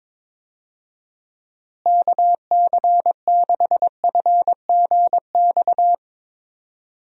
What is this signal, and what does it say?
Morse code, heard as a keyed tone, about 22 wpm, 700 Hz. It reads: KC6FGX
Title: USA Call Signs - 6 Characters - Repeated 3x - 22wpm